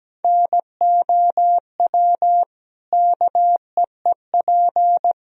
Morse code, heard as a keyed tone, about 17 wpm, 700 Hz. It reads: NOW KEEP
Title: Sentences from Top 1000 Words - Review Part 2 - 17wpm